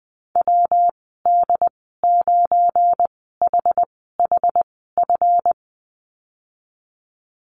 Morse code, heard as a keyed tone, about 20 wpm, spaced as for 15 wpm, 700 Hz. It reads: WD9HHF
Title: 173 Lesson 28g - Number 9 - Call Signs 20wpm